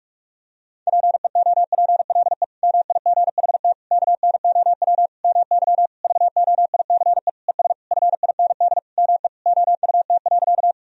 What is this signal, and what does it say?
Morse code, heard as a keyed tone, about 46 wpm, 690 Hz. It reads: PEOPLE MIGHT KNOW MY VOICE ES FIND ME OUT.
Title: Sentences from Top 500 Words and Abbreviations 45wpm